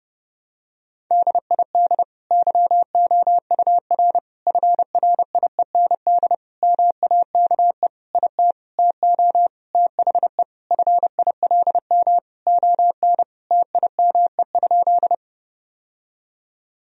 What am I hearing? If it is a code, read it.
Morse code, heard as a keyed tone, about 30 wpm, 700 Hz. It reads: DID YOUR FRIEND MAKE IT TO THE FILM ON TIME?
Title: Sentences from Top 1000 Words - Review Part 3 - 30wpm